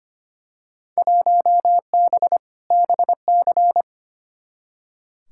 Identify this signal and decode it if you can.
Morse code, heard as a keyed tone, about 25 wpm, 700 Hz. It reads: 1B BC
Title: Field Day Exchanges - 25wpm